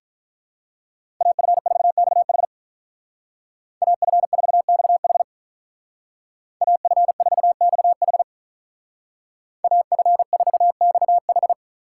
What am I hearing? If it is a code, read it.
Morse code, heard as a keyed tone, about 47 wpm, 700 Hz. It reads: AF4XH AF4XH AF4XH AF4XH